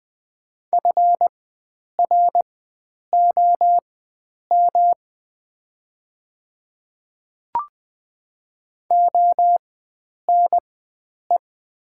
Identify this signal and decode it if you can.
Morse code, heard as a keyed tone, about 20 wpm, spaced as for 9 wpm, 700 Hz. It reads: FROM ONE
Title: 4x Character Spacing - Top 300 Words - 20wpm